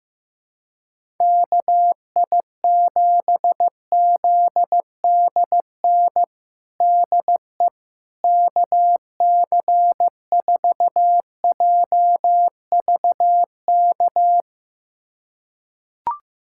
Morse code, heard as a keyed tone, about 15 wpm, 700 Hz. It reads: KI7ZDN DE KC4JVK
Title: USA Callsign DE Callsign - 6 Characters - 15wpm